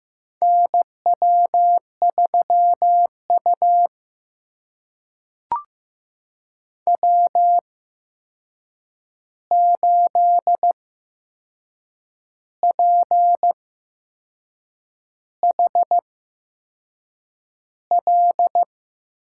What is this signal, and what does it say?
Morse code, heard as a keyed tone, about 15 wpm, spaced as for 6 wpm, 700 Hz. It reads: NW3U W8PHL
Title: USA Call Signs - 8x Character Spacing - 15wpm